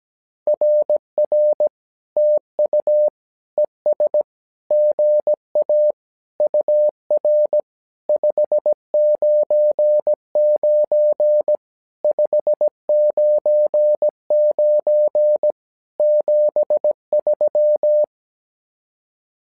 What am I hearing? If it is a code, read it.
Morse code, heard as a keyed tone, about 17 wpm, 600 Hz. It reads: RR TU ES GA UR 599 599 73
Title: SOTA QSO Practice 17wpm